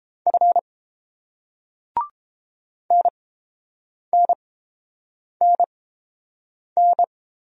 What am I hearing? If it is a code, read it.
Morse code, heard as a keyed tone, about 30 wpm, spaced as for 7 wpm, 690 Hz. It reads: F NNNN